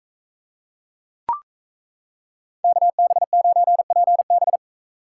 Morse code, heard as a keyed tone, about 34 wpm, 700 Hz. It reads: KB9PB
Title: Speed Racing - USA 2x2 Call Signs 28wpm